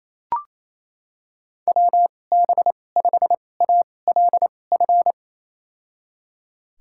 Morse code, heard as a keyed tone, about 28 wpm, spaced as for 20 wpm, 700 Hz. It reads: WB5ALF
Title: USA Call Signs - 6 Characters - 2x Character Spacing - 28wpm